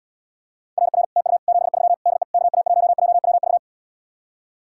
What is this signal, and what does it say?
Morse code, heard as a keyed tone, about 80 wpm, 700 Hz. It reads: RR TU 73 DE KA1JWV